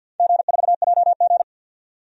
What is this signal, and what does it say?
Morse code, heard as a keyed tone, about 52 wpm, 700 Hz. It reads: G4JG
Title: DXCC-100 Call Signs - 4 Characters - 50wpm